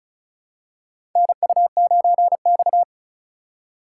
Morse code, heard as a keyed tone, about 36 wpm, 700 Hz. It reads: NU9X
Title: USA Call Signs - 4 Characters - 4x Character Spacing - 35wpm